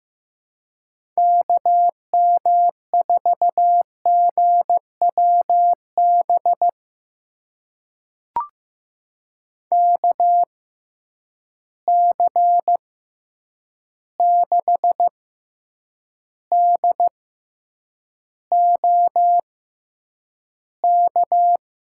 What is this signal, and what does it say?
Morse code, heard as a keyed tone, about 15 wpm, spaced as for 8 wpm, 700 Hz. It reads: KM4GWB KC6DOK
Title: USA Call Signs - 6 Characters - 6x Character Spacing - 15wpm